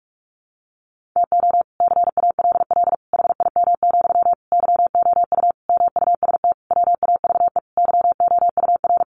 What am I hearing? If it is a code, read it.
Morse code, heard as a keyed tone, about 44 wpm, 700 Hz. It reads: TO CALL HIM, YOU MUST WAVE YOUR
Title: Sentences from Top 1400 Words 45wpm